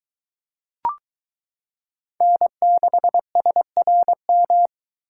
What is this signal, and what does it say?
Morse code, heard as a keyed tone, about 22 wpm, 700 Hz. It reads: N6SRM